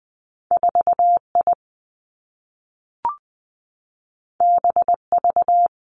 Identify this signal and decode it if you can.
Morse code, heard as a keyed tone, about 20 wpm, 700 Hz. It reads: 4 I B V
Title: Two Challenging Letter-Number Combinations 20wpm